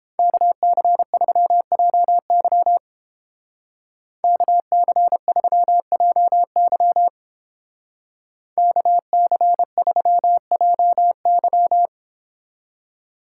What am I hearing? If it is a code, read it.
Morse code, heard as a keyed tone, about 30 wpm, 700 Hz. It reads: KC3JY KC3JY KC3JY